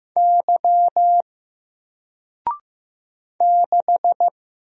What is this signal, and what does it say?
Morse code, heard as a keyed tone, about 15 wpm, 700 Hz. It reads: Y 6
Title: Single Letter-Number Alt 15wpm